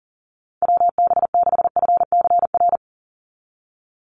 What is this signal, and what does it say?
Morse code, heard as a keyed tone, about 40 wpm, 700 Hz. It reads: WB6FCR
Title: USA Call Signs 40wpm